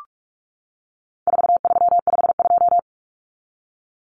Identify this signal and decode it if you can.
Morse code, heard as a keyed tone, about 44 wpm, 700 Hz. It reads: 4352
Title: Common Cut Numbers - 4-Digit 45wpm